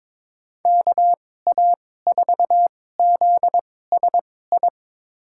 Morse code, heard as a keyed tone, about 22 wpm, spaced as for 16 wpm, 700 Hz. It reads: KA4ZSI
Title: USA Call Signs - 6 Characters - 2x Character Spacing - 22wpm